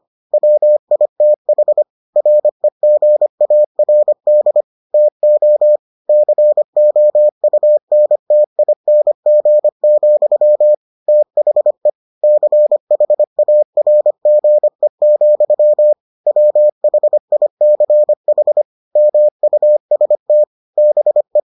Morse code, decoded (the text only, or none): WITH REGARD TO COUNTING, THE CHARGE, WHICH MUST BE